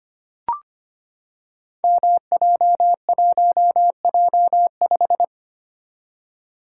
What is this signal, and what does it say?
Morse code, heard as a keyed tone, about 25 wpm, 700 Hz. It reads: MJ1J5
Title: Code Groups - 5 Letter:Numbers - 25wpm